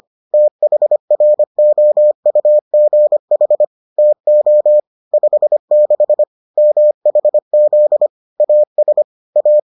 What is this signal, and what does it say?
Morse code, heard as a keyed tone, about 25 wpm, 600 Hz. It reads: THROUGH TO 56 MHZ AS A